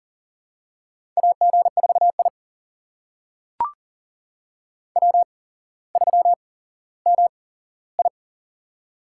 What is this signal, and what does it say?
Morse code, heard as a keyed tone, about 40 wpm, spaced as for 17 wpm, 700 Hz. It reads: AG4I W3MI
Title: USA Call Signs - 4 Characters - 8x Character Spacing - 40wpm